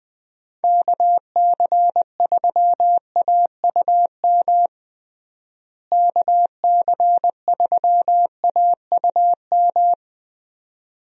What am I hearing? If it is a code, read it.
Morse code, heard as a keyed tone, about 20 wpm, 700 Hz. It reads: KC3AUM KC3AUM
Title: USA Call Signs - 6 Characters - Repeated 2x - 20wpm